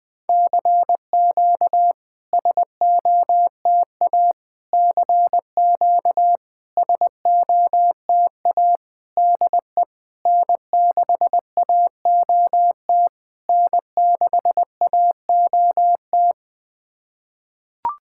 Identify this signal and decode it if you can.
Morse code, heard as a keyed tone, about 20 wpm, 700 Hz. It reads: CQ SOTA CQ SOTA DE N6AOT N6AOT
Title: SOTA QSO Practice 20wpm